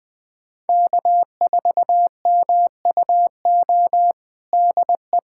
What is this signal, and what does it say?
Morse code, heard as a keyed tone, about 20 wpm, 700 Hz. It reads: K4MUO DE
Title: USA Callsign DE Callsign - 5 Characters - 20wpm